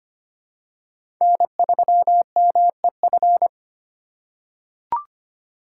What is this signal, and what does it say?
Morse code, heard as a keyed tone, about 25 wpm, 700 Hz. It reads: N3MEF
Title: USA Call Signs - 5 Characters - 2x Character Spacing - 25wpm